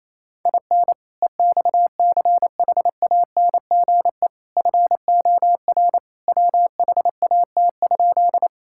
Morse code, heard as a keyed tone, about 28 wpm, 700 Hz. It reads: IN EXCHANGE FOR WHAT?
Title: Sentences from Top 1000 Words - Review Part 6 - 28wpm